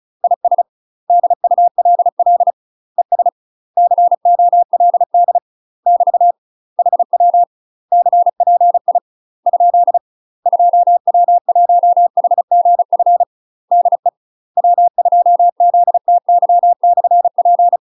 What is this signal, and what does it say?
Morse code, heard as a keyed tone, about 34 wpm, 700 Hz. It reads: IS DULL ES COLD <BT> HW CPI ? 2W1HGF DE W2ZTY/P
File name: UK MASTER QSOs TRAINING / 35WPM / UK MASTER QSOs 0039